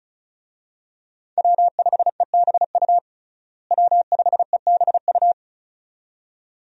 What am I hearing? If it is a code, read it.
Morse code, heard as a keyed tone, about 34 wpm, 700 Hz. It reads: W5EBU W5EBU
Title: USA Call Signs - 5 Characters - Repeated 2x - 35wpm